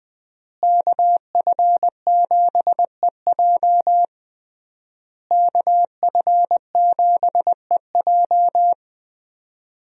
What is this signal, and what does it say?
Morse code, heard as a keyed tone, about 19 wpm, 700 Hz. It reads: KF7EJ KF7EJ
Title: USA Call Signs - 5 Characters - Repeated 3x - 20wpm